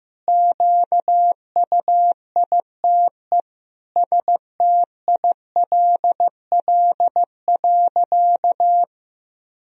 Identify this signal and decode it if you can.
Morse code, heard as a keyed tone, about 15 wpm, 700 Hz. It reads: QUITE STILL.
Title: Sentences from Top 1000 Words - Review Part 2 - 15wpm